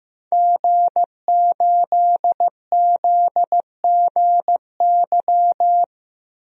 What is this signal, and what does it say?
Morse code, heard as a keyed tone, about 16 wpm, 700 Hz. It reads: G8ZGY
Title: DXCC-100 Call Signs - 5 Characters - 15wpm